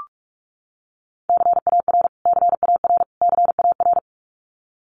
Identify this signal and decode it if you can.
Morse code, heard as a keyed tone, about 44 wpm, 700 Hz. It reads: CAR CAR CAR
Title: Top 500 Words Repeated 45wpm